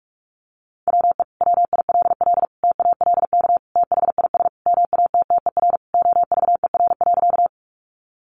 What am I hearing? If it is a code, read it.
Morse code, heard as a keyed tone, about 44 wpm, 700 Hz. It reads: WE WILL TALK THIS MATTER OVER.
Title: Sentences from Top 700 Words 45wpm